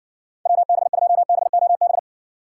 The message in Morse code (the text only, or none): WB2BOB